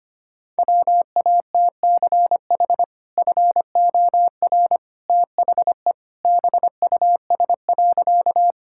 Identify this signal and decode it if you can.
Morse code, heard as a keyed tone, about 25 wpm, 700 Hz. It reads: WATCH FOR THE BUS.
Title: Sentences from Top 1000 Words - Review Part 6 - 25wpm